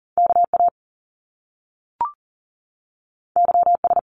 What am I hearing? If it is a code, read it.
Morse code, heard as a keyed tone, about 40 wpm, 700 Hz. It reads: KA YS
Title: Top Two Letter Combinations in Words 40wpm